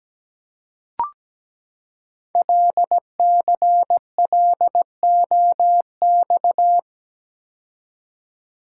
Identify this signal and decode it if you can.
Morse code, heard as a keyed tone, about 17 wpm, 700 Hz. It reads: LCLOX
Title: Code Groups - 5 Letter:Numbers - 17wpm